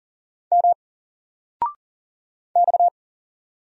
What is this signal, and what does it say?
Morse code, heard as a keyed tone, about 39 wpm, 700 Hz. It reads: M X
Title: Single Character (Letter:Number:Punctuation:Prosign) Alt 40wpm